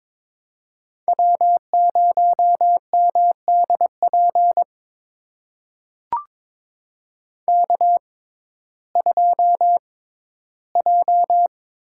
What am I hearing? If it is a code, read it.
Morse code, heard as a keyed tone, about 22 wpm, spaced as for 14 wpm, 700 Hz. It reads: W0MDP K2J
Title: USA Call Signs - 5 Characters - 6x Character Spacing - 22wpm